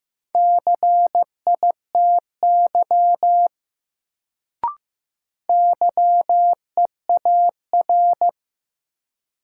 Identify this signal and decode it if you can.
Morse code, heard as a keyed tone, about 15 wpm, 700 Hz. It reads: CITY YEAR
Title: Sets of 2 Words - New Word Drill - 15wpm